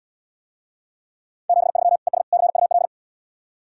Morse code, heard as a keyed tone, about 75 wpm, 700 Hz. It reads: 73 ES CUL